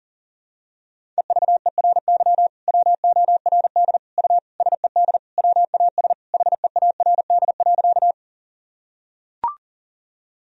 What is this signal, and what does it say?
Morse code, heard as a keyed tone, about 37 wpm, 700 Hz. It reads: EVERY WORD U SED WAS HEARD.
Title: Sentences from Top 400 Words and Abbreviations 40wpm